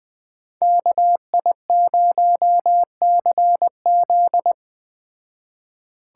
Morse code, heard as a keyed tone, about 21 wpm, 700 Hz. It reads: KI0CZ